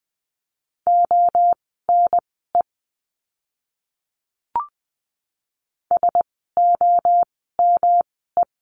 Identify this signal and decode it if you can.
Morse code, heard as a keyed tone, about 20 wpm, spaced as for 15 wpm, 700 Hz. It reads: ONE SOME
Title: 139 Lesson 23f - Letter f - Words 20wpm